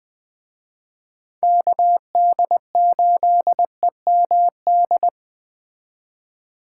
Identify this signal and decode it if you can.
Morse code, heard as a keyed tone, about 20 wpm, 700 Hz. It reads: KD8EMD